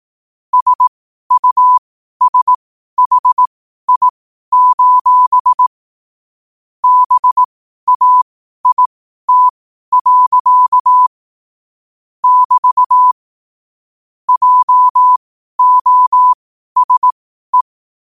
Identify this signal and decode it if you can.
Morse code, heard as a keyed tone, about 18 wpm, spaced as for 12 wpm, 1000 Hz. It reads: SUSHI: BAIT. <BT> JOSE
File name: qotdfiles / 12wpm